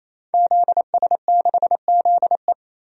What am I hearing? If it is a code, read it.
Morse code, heard as a keyed tone, about 27 wpm, 700 Hz. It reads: ZS6ZE
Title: DXCC-100 Call Signs - 5 Characters - 28wpm